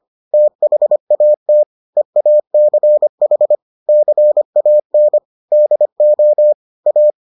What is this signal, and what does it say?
Morse code, heard as a keyed tone, about 25 wpm, 600 Hz. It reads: THAT EACH CAN DO A